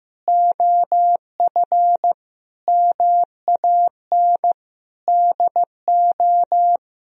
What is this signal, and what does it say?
Morse code, heard as a keyed tone, about 15 wpm, 700 Hz. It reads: OF MAN DO
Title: Sentences from Top 1000 Words - Review Part 3 - 15wpm